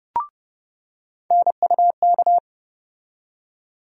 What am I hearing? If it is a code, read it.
Morse code, heard as a keyed tone, about 31 wpm, 700 Hz. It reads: NUK
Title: USA Call Sign Postfixes - 3 Letters 30wpm